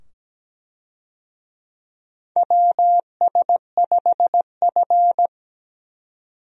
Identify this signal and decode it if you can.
Morse code, heard as a keyed tone, about 17 wpm, 700 Hz. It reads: WS5F
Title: USA Call Signs - 4 Characters - 17wpm